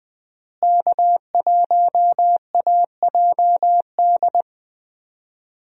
Morse code, heard as a keyed tone, about 20 wpm, 700 Hz. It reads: K1AJD
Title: USA Call Signs - 5 Characters - Repeated 2x - 20wpm